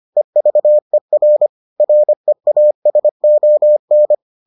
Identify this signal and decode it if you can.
Morse code, heard as a keyed tone, about 25 wpm, 600 Hz. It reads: EVER REASON